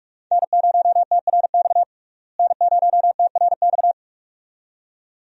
Morse code, heard as a keyed tone, about 43 wpm, 700 Hz. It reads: N0TRX N0TRX
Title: USA Call Signs - 5 Characters - Repeated 2x - 45wpm